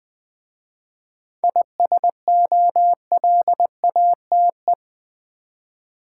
Morse code, heard as a keyed tone, about 20 wpm, 700 Hz. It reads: ISOLATE
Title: 7 Letter Words - First 3 Letters Given - 20wpm